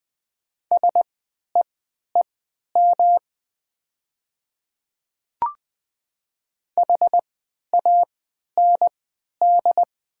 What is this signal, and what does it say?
Morse code, heard as a keyed tone, about 20 wpm, spaced as for 11 wpm, 700 Hz. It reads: SEEM HAND
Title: 3x Character Spacing - Top 300 Words - 20wpm